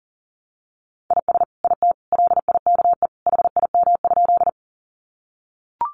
Morse code, heard as a keyed tone, about 37 wpm, 700 Hz. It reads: IS IT LIKE HIM?
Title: Sentences from Top 100 Words 40wpm